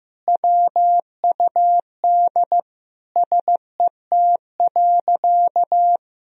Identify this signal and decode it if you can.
Morse code, heard as a keyed tone, about 15 wpm, 700 Hz. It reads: WUD SET.
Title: Sentences from Top 400 Words and Abbreviations 15wpm